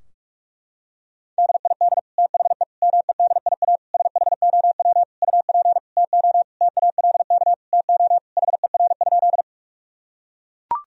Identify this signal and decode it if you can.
Morse code, heard as a keyed tone, about 41 wpm, 700 Hz. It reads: DID THE MEDIA SHOW UP TO TALK TO HER?
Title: Sentences from Top 1000 Words - Review Part 4 - 45wpm